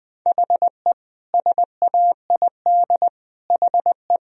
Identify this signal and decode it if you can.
Morse code, heard as a keyed tone, about 21 wpm, 700 Hz. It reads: HE SAID HE
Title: Sentences from Top 1000 Words - Review Part 3 - 20wpm